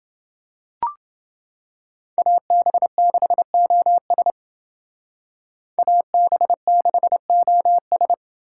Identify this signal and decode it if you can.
Morse code, heard as a keyed tone, about 28 wpm, 700 Hz. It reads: AB6OS AB6OS